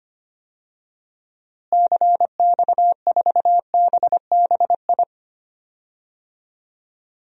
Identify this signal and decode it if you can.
Morse code, heard as a keyed tone, about 25 wpm, 700 Hz. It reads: CX4BBI
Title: DXCC-100 Call Signs - 6 Characters - Repeated 2x - 25wpm